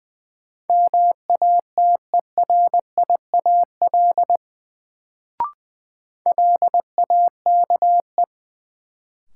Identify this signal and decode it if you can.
Morse code, heard as a keyed tone, about 20 wpm, 700 Hz. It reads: MATERIAL LAKE
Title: Sentences from Top 1100 Words - New Word Drill Alt - 20wpm